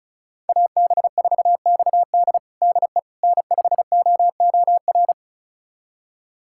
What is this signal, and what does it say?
Morse code, heard as a keyed tone, about 36 wpm, 700 Hz. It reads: AB4XD DE N5OOR
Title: USA Callsign DE Callsign - 5 Characters - 35wpm